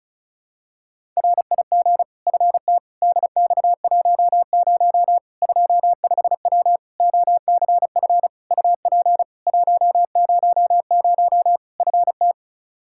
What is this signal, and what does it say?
Morse code, heard as a keyed tone, about 36 wpm, 700 Hz. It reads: RIG FT DX10 25W OCF UP 100 FT
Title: Practice QSO - Exchange 2 - Info - 35wpm